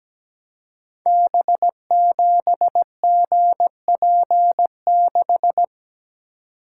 Morse code, heard as a keyed tone, about 17 wpm, 700 Hz. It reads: B7GP6
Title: Code Groups - 5 Letter:Numbers - 17wpm